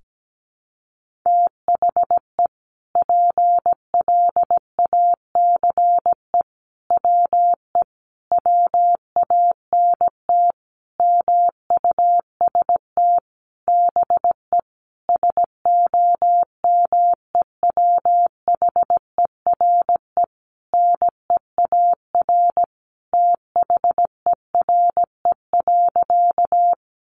Morse code, heard as a keyed tone, about 17 wpm, 700 Hz. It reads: THE PLACE WE WANT MUST BE SOMEWHERE NEAR THERE.
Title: Sentences from Top 1400 Words 17wpm